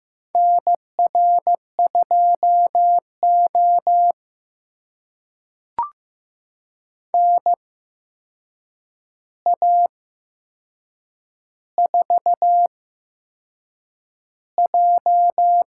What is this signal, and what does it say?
Morse code, heard as a keyed tone, about 15 wpm, spaced as for 6 wpm, 700 Hz. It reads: NR2O NA4J
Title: USA Call Signs - 4 Characters - 8x Character Spacing - 15wpm